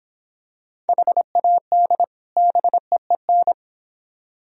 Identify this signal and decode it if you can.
Morse code, heard as a keyed tone, about 26 wpm, 700 Hz. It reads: HAD BEEN